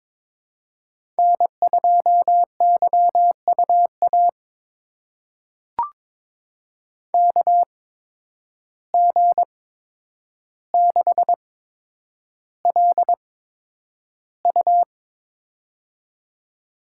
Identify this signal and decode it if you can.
Morse code, heard as a keyed tone, about 22 wpm, spaced as for 9 wpm, 700 Hz. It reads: N2YUA KG6LU
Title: USA Call Signs - 8x Character Spacing - 22wpm